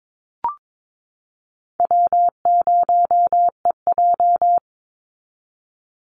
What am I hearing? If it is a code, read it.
Morse code, heard as a keyed tone, about 22 wpm, 700 Hz. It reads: W0EJ
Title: Easy Call Signs Repeated 3x - 22wpm